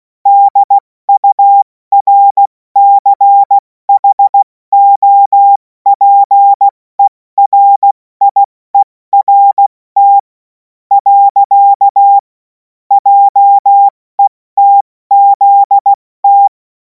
Morse code, heard as a keyed tone, about 16 wpm, spaced as for 14 wpm, 800 Hz. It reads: DURCHOPERIERT. JETZT
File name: Morsekurs / Witze / Witz 11. August 2026 70 F80